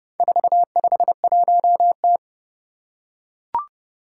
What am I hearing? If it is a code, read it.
Morse code, heard as a keyed tone, about 30 wpm, 700 Hz. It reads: 451T
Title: Four Challenging Letter-Number Combinations 30wpm